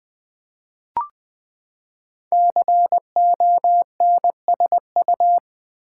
Morse code, heard as a keyed tone, about 20 wpm, 700 Hz. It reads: CONSU
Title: Build A Word - 5 Letters At A Time - 12-Letter Words - 20wpm